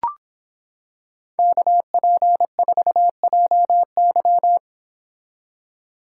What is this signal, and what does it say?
Morse code, heard as a keyed tone, about 26 wpm, 700 Hz. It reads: KP4JY